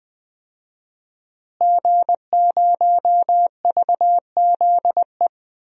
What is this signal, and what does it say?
Morse code, heard as a keyed tone, about 20 wpm, 700 Hz. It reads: G0VZE
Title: DXCC-100 Call Signs - 5 Characters - Repeated 3x - 20wpm